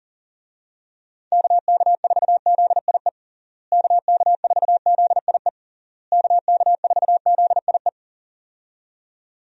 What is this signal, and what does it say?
Morse code, heard as a keyed tone, about 40 wpm, 700 Hz. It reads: KK4ZIE KK4ZIE KK4ZIE
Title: USA Call Signs - 6 Characters - Repeated 3x - 40wpm